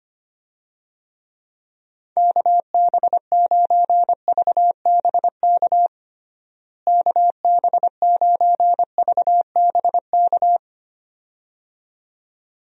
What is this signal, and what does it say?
Morse code, heard as a keyed tone, about 25 wpm, 700 Hz. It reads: KB9VBK KB9VBK
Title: USA Call Signs - 6 Characters - Repeated 2x - 25wpm